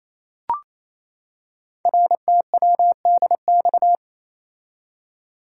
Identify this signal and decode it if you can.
Morse code, heard as a keyed tone, about 27 wpm, 700 Hz. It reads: RTWDX
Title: Code Groups - 5 Letters - 28wpm